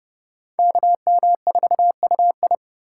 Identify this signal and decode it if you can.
Morse code, heard as a keyed tone, about 28 wpm, 700 Hz. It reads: KM4UI